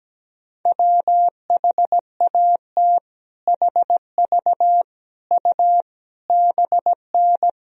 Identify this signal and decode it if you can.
Morse code, heard as a keyed tone, about 17 wpm, 700 Hz. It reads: WHAT HV U BN
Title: Sentences from Top 200 Words and Abbreviations 17wpm